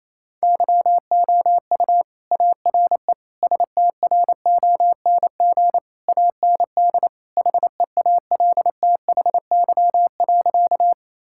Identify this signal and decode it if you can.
Morse code, heard as a keyed tone, about 27 wpm, 700 Hz. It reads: YOU ARE STRONG AND HEALTHY.
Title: Sentences from Top 1000 Words - Review Part 5 - 28wpm